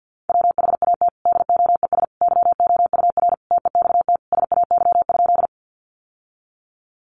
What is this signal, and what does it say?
Morse code, heard as a keyed tone, about 48 wpm, 700 Hz. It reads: WHAT DOES YOUR TEXT SAY?
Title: Sentences from Top 1200 Words 50wpm